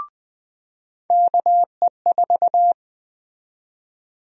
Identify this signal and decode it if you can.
Morse code, heard as a keyed tone, about 20 wpm, 700 Hz. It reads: KE4
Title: USA Call Sign Prefixes - 20wpm